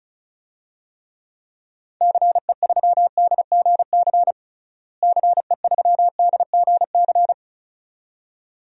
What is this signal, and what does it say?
Morse code, heard as a keyed tone, about 36 wpm, 700 Hz. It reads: CE3DGC CE3DGC
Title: DXCC-100 Call Signs - 6 Characters - Repeated 2x - 35wpm